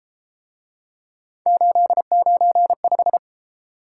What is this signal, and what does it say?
Morse code, heard as a keyed tone, about 32 wpm, 700 Hz. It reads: 895